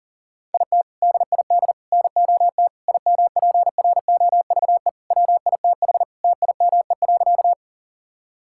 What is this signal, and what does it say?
Morse code, heard as a keyed tone, about 37 wpm, 690 Hz. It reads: IT DID NOT IMPROVE WITH TIME.
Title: Sentences from Top 1000 Words - Review Part 5 - 40wpm